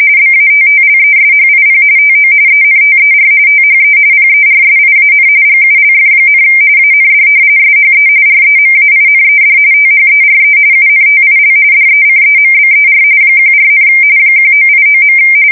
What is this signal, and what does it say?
RTTY, about 45.45 baud. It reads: BATTLESHIP TO ENTER THE PERSIAN GULF, AND SHE REMAINED THERE UNTIL DECEMBER 14, 1989.
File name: fsk170 / battleshipnj